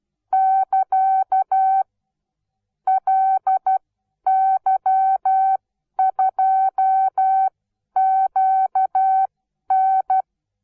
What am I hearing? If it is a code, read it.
Morse code, heard as a keyed tone, about 12 wpm, spaced as for 10 wpm, 760 Hz. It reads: <KA> LY2QN
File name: Onlinelehrgang / morsen / DJ4UF Morsen17